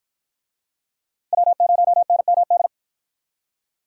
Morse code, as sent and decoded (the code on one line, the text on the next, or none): .-- ----- -. -- -..
W0NMD